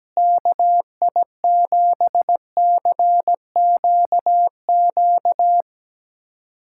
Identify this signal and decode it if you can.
Morse code, heard as a keyed tone, about 17 wpm, 700 Hz. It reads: KI7CQQ
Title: USA Callsign DE Callsign 17wpm